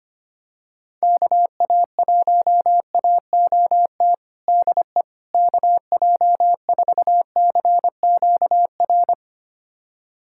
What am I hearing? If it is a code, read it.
Morse code, heard as a keyed tone, about 25 wpm, 700 Hz. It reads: KA1AOT DE KJ4CQR
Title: USA Callsign DE Callsign 25wpm